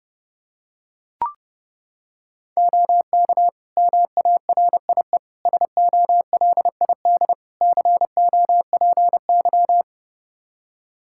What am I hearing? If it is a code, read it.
Morse code, heard as a keyed tone, about 30 wpm, 700 Hz. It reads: OK MARIE SOLID COPY
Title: Practice QSO - Exchange 3 - More Info - 30wpm